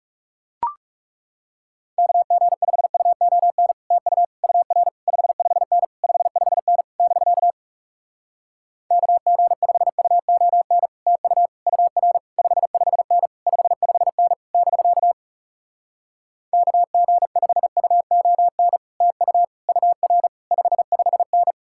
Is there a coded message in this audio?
Yes: Morse code from 2.0 to 21.5 s, heard as a keyed tone, about 40 wpm, 700 Hz. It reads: KG5UON TU UR 55N 55N <BK> KG5UON TU UR 55N 55N <BK> KG5UON TU UR 55N